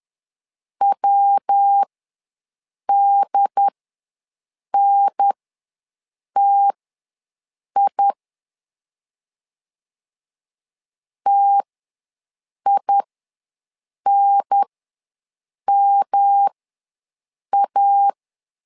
Morse code, heard as a keyed tone, about 10 wpm, spaced as for 6 wpm, 800 Hz. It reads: WDNTI TINMA
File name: LEZIONI / 3 MWRO / 3 MW ws 10wpm 2213